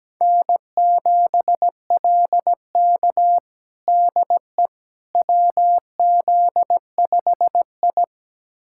Morse code, heard as a keyed tone, about 17 wpm, 700 Hz. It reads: N7LK DE WZ5I
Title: USA Callsign DE Callsign - 4 Characters - 17wpm